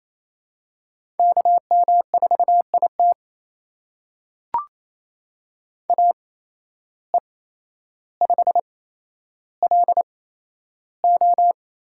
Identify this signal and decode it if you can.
Morse code, heard as a keyed tone, about 27 wpm, spaced as for 12 wpm, 700 Hz. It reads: KM4IT AE5LO
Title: USA Call Signs - 5 Characters - 8x Character Spacing - 28wpm